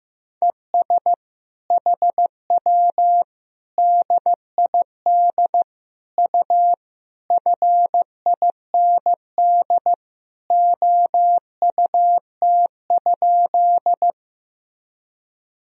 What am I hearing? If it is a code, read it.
Morse code, heard as a keyed tone, about 16 wpm, 700 Hz. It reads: ES HW DID U FIND OUT?
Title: Sentences from Top 200 Words and Abbreviations 15wpm